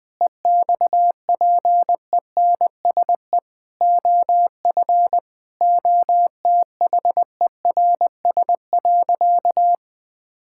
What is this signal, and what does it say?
Morse code, heard as a keyed tone, about 20 wpm, 700 Hz. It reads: EXPENSE OF OTHERS.
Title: Sentences with 4-Word Phrases - Spoken Beginning - 20wpm